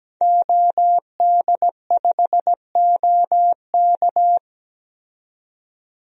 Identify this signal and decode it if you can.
Morse code, heard as a keyed tone, about 17 wpm, 700 Hz. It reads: OD5OK
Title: DXCC-100 Call Signs - 5 Characters - 17wpm